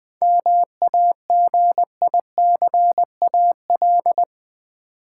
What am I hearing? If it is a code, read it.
Morse code, heard as a keyed tone, about 20 wpm, 700 Hz. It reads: MAGICAL
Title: 7 Letter Words - First 3 Letters Given - 20wpm